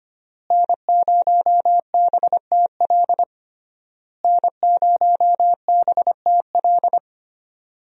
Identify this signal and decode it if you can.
Morse code, heard as a keyed tone, about 25 wpm, 700 Hz. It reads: N0BTL N0BTL
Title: USA Call Signs - 5 Characters - Repeated 3x - 25wpm